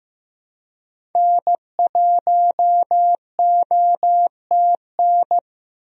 Morse code, heard as a keyed tone, about 15 wpm, 700 Hz. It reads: N1OTN
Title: USA Call Signs - 5 Characters - Repeated 2x - 15wpm